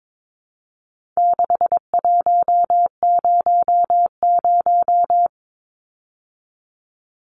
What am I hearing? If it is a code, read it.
Morse code, heard as a keyed tone, about 22 wpm, 700 Hz. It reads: 6100
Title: Numbers - 4-Digit 22wpm